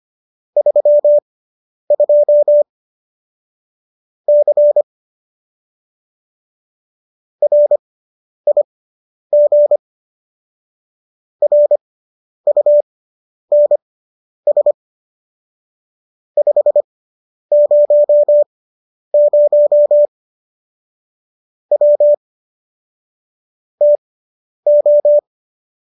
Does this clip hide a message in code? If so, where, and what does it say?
Yes: Morse code from 0.6 to 25.2 s, heard as a keyed tone, about 25 wpm, spaced as for 10 wpm, 590 Hz. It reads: 32 C RIG RUNS 500 W TO